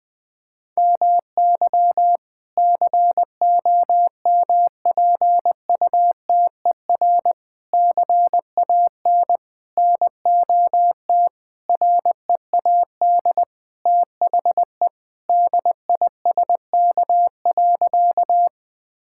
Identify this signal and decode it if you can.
Morse code, heard as a keyed tone, about 21 wpm, 700 Hz. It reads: MY COMPUTER CAN NOT READ THE DISK.
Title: Sentences from Top 1000 Words - Review Part 6 - 20wpm